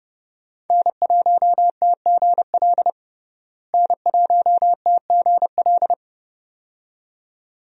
Morse code, heard as a keyed tone, about 30 wpm, 700 Hz. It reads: N1TGL N1TGL
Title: USA Call Signs - 5 Characters - Repeated 2x - 30wpm